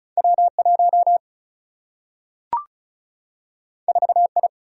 Morse code, heard as a keyed tone, about 36 wpm, 700 Hz. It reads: W 1 4 I
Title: Single-Letter Single-Number 35wpm